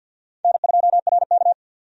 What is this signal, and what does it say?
Morse code, heard as a keyed tone, about 47 wpm, 700 Hz. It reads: N2RK
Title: USA Call Signs - 4 Characters - 50wpm